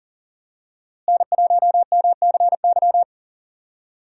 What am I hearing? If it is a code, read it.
Morse code, heard as a keyed tone, about 38 wpm, 700 Hz. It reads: N1MCY
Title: USA Call Signs - 5 Characters - Repeated 3x - 40wpm